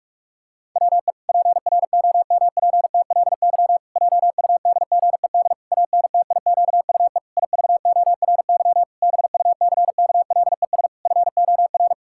Morse code, heard as a keyed tone, about 41 wpm, 700 Hz. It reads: WE PROMPTLY JUDGED ANTIQUE IVORY BUCKLES FOR
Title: Pangram Challenge v2 - 45wpm